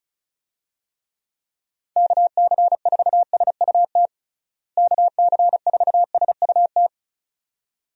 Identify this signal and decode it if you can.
Morse code, heard as a keyed tone, about 34 wpm, 700 Hz. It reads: KC4SUT KC4SUT
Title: USA Call Signs - 6 Characters - Repeated 2x - 35wpm